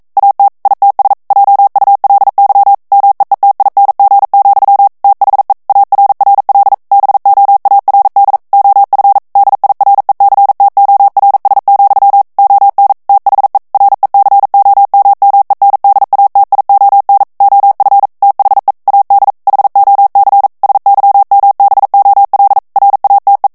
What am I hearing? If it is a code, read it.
Morse code, heard as a keyed tone, about 42 wpm, 800 Hz. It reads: AT ITS JULY MEETING, THE ARRL BOARD OF DIRECTORS, ON THE RECOMMENDATION OF THE AD HOC SYMBOL RATE